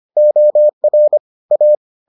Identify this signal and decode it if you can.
Morse code, heard as a keyed tone, about 25 wpm, 600 Hz. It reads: OR A